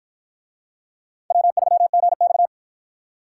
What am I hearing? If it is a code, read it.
Morse code, heard as a keyed tone, about 46 wpm, 700 Hz. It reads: W3GX